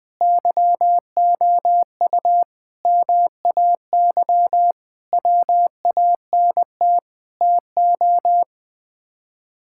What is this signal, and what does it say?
Morse code, heard as a keyed tone, about 20 wpm, 700 Hz. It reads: YOU MAY WANT TO
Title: Sets of 8 Words - 20wpm